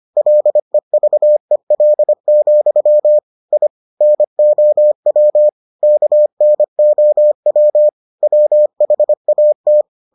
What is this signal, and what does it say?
Morse code, heard as a keyed tone, about 25 wpm, 600 Hz. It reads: LEVEL, I NOW KNOW WHAT